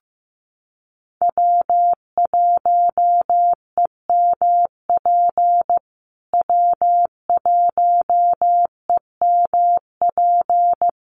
Morse code, heard as a keyed tone, about 15 wpm, 700 Hz. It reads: W1EMP W1EMP
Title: USA Call Signs Repeated 3x 15wpm